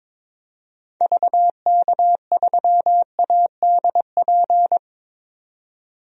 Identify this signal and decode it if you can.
Morse code, heard as a keyed tone, about 22 wpm, 700 Hz. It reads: VK3ADP
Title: DXCC-100 Call Signs - 6 Characters - 22wpm